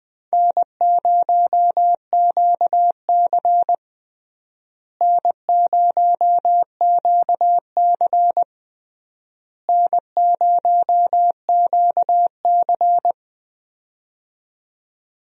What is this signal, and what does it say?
Morse code, heard as a keyed tone, about 20 wpm, 700 Hz. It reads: N0QC N0QC N0QC
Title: USA Call Signs - 4 Characters - Repeated 3x - 20wpm